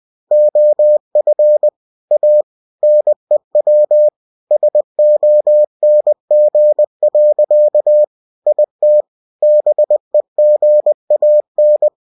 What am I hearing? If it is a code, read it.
Morse code, heard as a keyed tone, about 20 wpm, 600 Hz. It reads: OF A NEW SONG. IT BEGAN